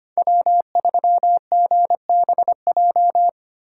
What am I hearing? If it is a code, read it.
Morse code, heard as a keyed tone, about 30 wpm, 700 Hz. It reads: W3GBJ
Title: Speed Racing - USA 1x3 Call Signs 25wpm